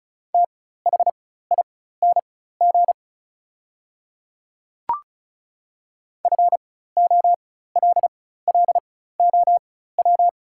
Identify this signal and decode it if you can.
Morse code, heard as a keyed tone, about 36 wpm, spaced as for 16 wpm, 700 Hz. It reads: THING FOLLOW
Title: 4x Character Spacing - Top 200 Words - 35wpm